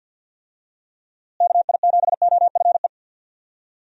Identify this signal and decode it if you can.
Morse code, heard as a keyed tone, about 48 wpm, 700 Hz. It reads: KI7OFE
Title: USA Call Signs - 6 Characters - Repeated 3x - 50wpm